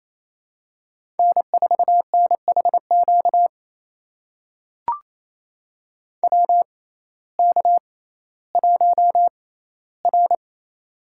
Morse code, heard as a keyed tone, about 28 wpm, spaced as for 15 wpm, 700 Hz. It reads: N4NHQ WK1R
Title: USA Call Signs - 6x Character Spacing - 28wpm